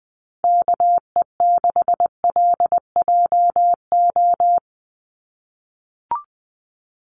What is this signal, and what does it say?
Morse code, heard as a keyed tone, about 20 wpm, 700 Hz. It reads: KE6LJO
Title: USA Call Signs 20wpm